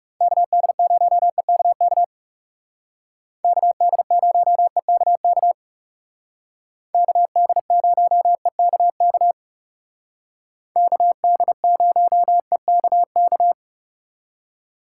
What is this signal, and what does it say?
Morse code, heard as a keyed tone, about 37 wpm, 700 Hz. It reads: KD0EKK KD0EKK KD0EKK KD0EKK
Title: Speed Racing - USA Call Signs 30wpm